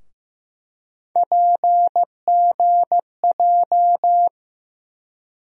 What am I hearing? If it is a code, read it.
Morse code, heard as a keyed tone, about 15 wpm, 700 Hz. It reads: PGJ
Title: USA Call Sign Postfixes - 3 Letters 15wpm